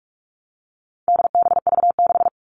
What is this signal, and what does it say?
Morse code, heard as a keyed tone, about 44 wpm, 700 Hz. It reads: DBV6
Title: All Cut Numbers - 4-Digit 45wpm